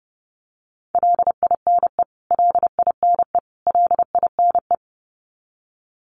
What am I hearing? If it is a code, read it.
Morse code, heard as a keyed tone, about 30 wpm, 700 Hz. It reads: LINE LINE LINE
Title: Top 200 Words Repeated 30wpm